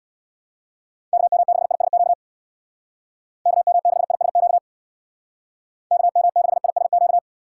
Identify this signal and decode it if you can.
Morse code, heard as a keyed tone, about 65 wpm, 700 Hz. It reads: KM6ISQ KM6ISQ KM6ISQ